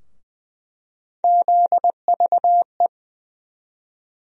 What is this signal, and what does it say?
Morse code, heard as a keyed tone, about 20 wpm, 700 Hz. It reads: ZVE
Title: USA Call Sign Postfixes - 3 Letters 20wpm